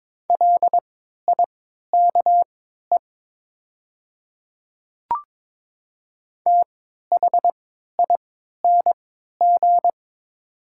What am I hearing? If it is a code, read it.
Morse code, heard as a keyed tone, about 22 wpm, spaced as for 13 wpm, 700 Hz. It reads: LIKE THING
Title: 3x Character Spacing - Top 100 Words - 22wpm